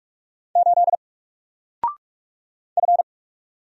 Morse code, heard as a keyed tone, about 46 wpm, 700 Hz. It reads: 8 F
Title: Single Letter-Number 45wpm